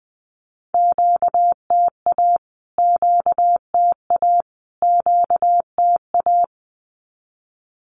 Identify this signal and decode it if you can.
Morse code, heard as a keyed tone, about 20 wpm, 700 Hz. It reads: QTA QTA QTA
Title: Q-Codes Repeated 3x 20wpm